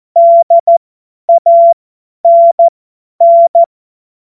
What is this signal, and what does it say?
Morse code, heard as a keyed tone, about 14 wpm, spaced as for 10 wpm, 680 Hz. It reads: DANN
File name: Morsekurs / Witze / Witz 12. April 2022 50 F70